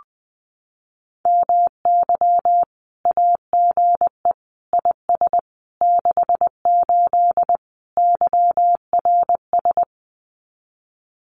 Morse code, heard as a keyed tone, about 20 wpm, 700 Hz. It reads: MY AGE IS 68 YRS
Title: Common QSO Phrases Encore 20wpm